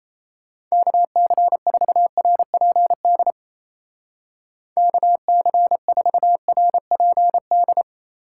Morse code, heard as a keyed tone, about 34 wpm, 700 Hz. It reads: KC4RPD KC4RPD
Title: Speed Racing - USA 2x3 Call Signs 28wpm